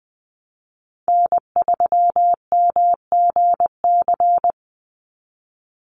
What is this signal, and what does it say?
Morse code, heard as a keyed tone, about 20 wpm, 700 Hz. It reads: N3MGC
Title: USA Call Signs 20wpm